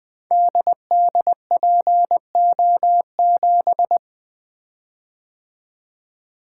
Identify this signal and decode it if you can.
Morse code, heard as a keyed tone, about 20 wpm, 700 Hz. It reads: DDPO7
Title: Code Groups - 5 Letter:Numbers - 20wpm